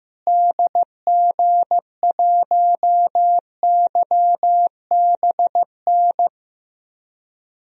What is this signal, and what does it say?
Morse code, heard as a keyed tone, about 15 wpm, 700 Hz. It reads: DG1YBN
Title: DXCC-100 Call Signs - 6 Characters - Repeated 2x - 15wpm